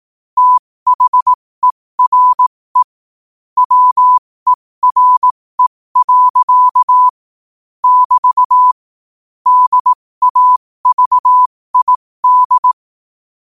Morse code, heard as a keyed tone, about 18 wpm, spaced as for 15 wpm, 1000 Hz. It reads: THERE WERE. <BT> DAVID
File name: qotdfiles / 15wpm